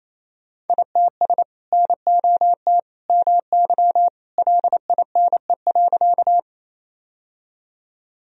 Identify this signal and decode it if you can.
Morse code, heard as a keyed tone, about 28 wpm, 700 Hz. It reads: ITS NOT MY LINE.
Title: Sentences from Top 1000 Words - Review Part 1 - 28wpm